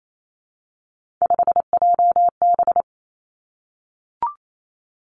Morse code, heard as a keyed tone, about 28 wpm, 700 Hz. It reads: 5JB
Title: Three Challenging Letter-Number Combinations 28wpm